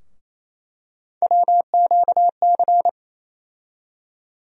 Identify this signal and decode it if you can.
Morse code, heard as a keyed tone, about 28 wpm, 700 Hz. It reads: WQC
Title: USA Call Sign Postfixes - 3 Letters 28wpm